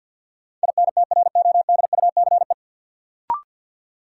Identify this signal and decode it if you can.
Morse code, heard as a keyed tone, about 52 wpm, 700 Hz. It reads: INTRODUCE
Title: Sentences from Top 1600 Words - New Word Drill - 50wpm